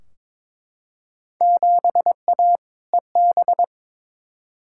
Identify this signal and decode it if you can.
Morse code, heard as a keyed tone, about 22 wpm, 700 Hz. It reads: 7A EB
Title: Field Day Exchanges Encore - 22wpm